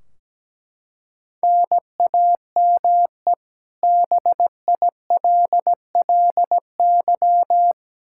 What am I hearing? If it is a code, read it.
Morse code, heard as a keyed tone, about 17 wpm, 700 Hz. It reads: NAME BILLY
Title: Practice QSO - Exchange 1 - Beginning - 17wpm